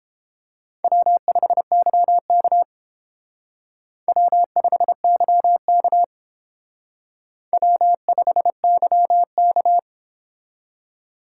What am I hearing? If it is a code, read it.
Morse code, heard as a keyed tone, about 30 wpm, 700 Hz. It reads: W5YK W5YK W5YK